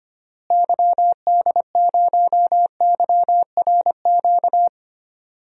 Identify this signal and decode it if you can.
Morse code, heard as a keyed tone, about 25 wpm, 700 Hz. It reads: YD0YRQ
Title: DXCC-100 Call Signs - 6 Characters - Repeated 3x - 25wpm